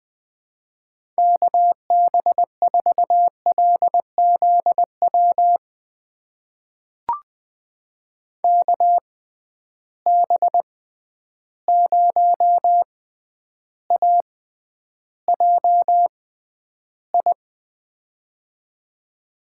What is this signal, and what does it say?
Morse code, heard as a keyed tone, about 20 wpm, spaced as for 10 wpm, 700 Hz. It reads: KB4LZW KB0AJI
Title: USA Call Signs - 6 Characters - 6x Character Spacing - 20wpm